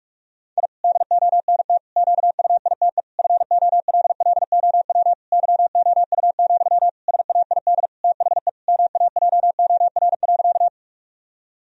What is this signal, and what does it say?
Morse code, heard as a keyed tone, about 46 wpm, 700 Hz. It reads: I DONT QUITE FOLLOW YOU, SAID THE MAJOR.
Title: Sentences from Top 1000 Words - Review Part 4 - 45wpm